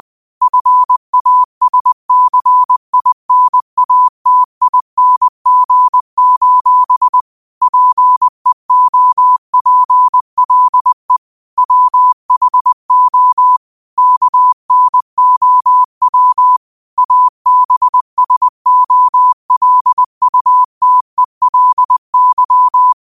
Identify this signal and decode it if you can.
Morse code, heard as a keyed tone, about 20 wpm, 1000 Hz. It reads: FASCINATING: PEOPLE WHO KNOW ABSOLUTELY